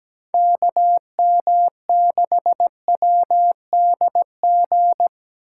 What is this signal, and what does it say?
Morse code, heard as a keyed tone, about 17 wpm, 700 Hz. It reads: KM6WDG
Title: USA Call Signs - 6 Characters - 3x Character Spacing - 17wpm